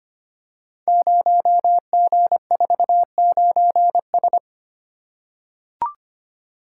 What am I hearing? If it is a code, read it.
Morse code, heard as a keyed tone, about 25 wpm, 700 Hz. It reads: 0G49S
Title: Code Groups - 5 Letter:Numbers - 25wpm